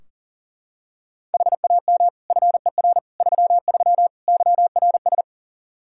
Morse code, heard as a keyed tone, about 40 wpm, 700 Hz. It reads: HAM FER 33 YRS
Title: Practice QSO - Exchange 3 - More Info - 40wpm